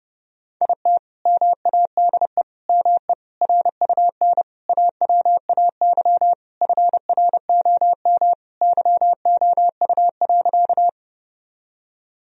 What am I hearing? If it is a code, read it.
Morse code, heard as a keyed tone, about 30 wpm, 700 Hz. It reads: IT MADE ME RUN AWAY FROM YOU.
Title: Sentences from Top 1000 Words - Review Part 1 - 30wpm